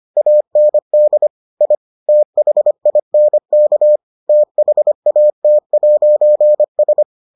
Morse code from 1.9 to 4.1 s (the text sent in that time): THINK